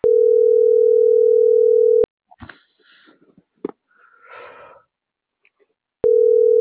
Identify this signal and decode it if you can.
Ringback tone (North America), 440 + 480 Hz.